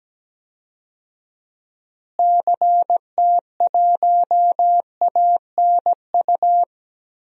Morse code, heard as a keyed tone, about 17 wpm, 700 Hz. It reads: CT1ANU
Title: DXCC-100 Call Signs - 6 Characters - Repeated 3x - 17wpm